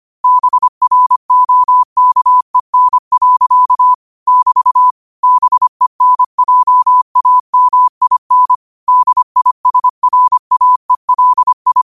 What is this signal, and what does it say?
Morse code, heard as a keyed tone, about 25 wpm, 1000 Hz. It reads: BROKEN. <BT> BENJAMIN DISRAELI
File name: qotdfiles / 25wpm